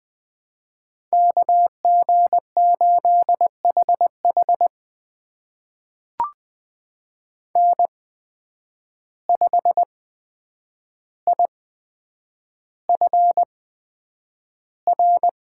Morse code, heard as a keyed tone, about 20 wpm, spaced as for 9 wpm, 700 Hz. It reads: KG8HH N5IFR
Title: USA Call Signs - 5 Characters - 8x Character Spacing - 20wpm